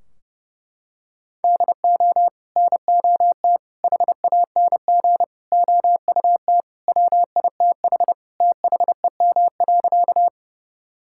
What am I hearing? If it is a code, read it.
Morse code, heard as a keyed tone, about 30 wpm, 700 Hz. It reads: DO NOT HANG OUT WITH THEM.
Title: Sentences from Top 1600 Words 30wpm